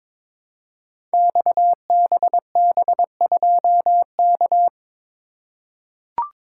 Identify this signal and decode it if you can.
Morse code, heard as a keyed tone, about 22 wpm, 700 Hz. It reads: XBB2K
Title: Code Groups - 5 Letter:Numbers - 22wpm